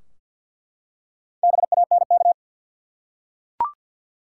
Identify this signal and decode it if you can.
Morse code, heard as a keyed tone, about 50 wpm, 700 Hz. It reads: BANK
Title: Four Letter Words - 50wpm